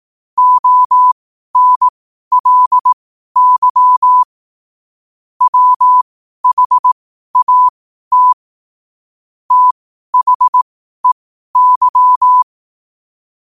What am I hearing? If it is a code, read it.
Morse code, heard as a keyed tone, about 18 wpm, spaced as for 12 wpm, 1000 Hz. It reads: ONLY WHAT THEY